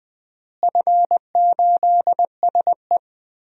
Morse code, heard as a keyed tone, about 20 wpm, 700 Hz. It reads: F8SE
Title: DXCC-100 Call Signs - 4 Characters - Repeated 2x - 20wpm